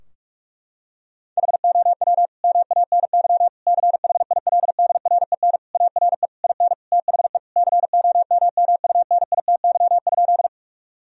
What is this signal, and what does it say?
Morse code, heard as a keyed tone, about 41 wpm, 700 Hz. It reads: HOW MANY CHILDREN ARE IN THE COMMUNITY?
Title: Sentences from Top 1000 Words - Review Part 4 - 45wpm